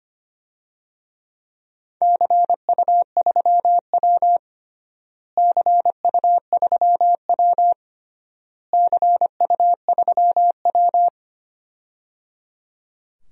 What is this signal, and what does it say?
Morse code, heard as a keyed tone, about 25 wpm, 700 Hz. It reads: CU3W CU3W CU3W
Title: DXCC-100 Call Signs - 4 Characters - Repeated 3x - 25wpm